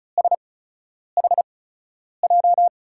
Morse code, heard as a keyed tone, about 36 wpm, spaced as for 10 wpm, 700 Hz. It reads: SHJ
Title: USA Call Signs - 6 Characters - 8x Character Spacing - 35wpm